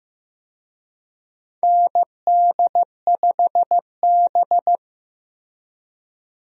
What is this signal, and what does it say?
Morse code, heard as a keyed tone, about 15 wpm, 700 Hz. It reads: ND5B
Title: USA Call Signs - 4 Characters - Repeated 3x - 15wpm